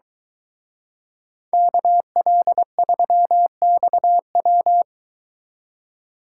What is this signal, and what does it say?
Morse code, heard as a keyed tone, about 22 wpm, 700 Hz. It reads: KL3XW